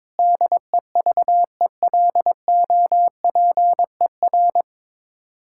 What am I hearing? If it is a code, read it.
Morse code, heard as a keyed tone, about 21 wpm, 700 Hz. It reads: DEVELOPER
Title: Practice QSO - Exchange 3 - More Info - 22wpm